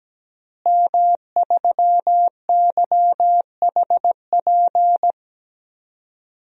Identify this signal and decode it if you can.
Morse code, heard as a keyed tone, about 17 wpm, 700 Hz. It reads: M3YHP
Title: DXCC-100 Call Signs - 5 Characters - Repeated 2x - 17wpm